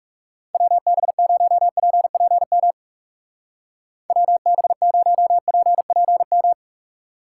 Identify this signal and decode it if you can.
Morse code, heard as a keyed tone, about 37 wpm, 700 Hz. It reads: WB0PPM WB0PPM
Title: Speed Racing - USA Call Signs 30wpm